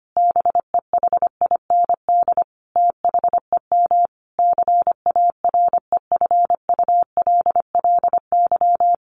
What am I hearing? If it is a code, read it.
Morse code, heard as a keyed tone, about 25 wpm, 700 Hz. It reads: BEHIND THEM CAREFULLY
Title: Sentences from Top 1300 Words 25wpm